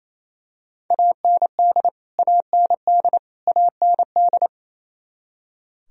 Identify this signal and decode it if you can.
Morse code, heard as a keyed tone, about 28 wpm, 700 Hz. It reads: AND AND AND
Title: Top 100 Words Repeated 28wpm